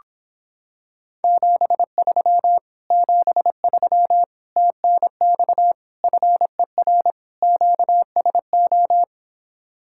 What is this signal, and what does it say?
Morse code, heard as a keyed tone, about 26 wpm, 700 Hz. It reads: 73 73 TNX FER QSO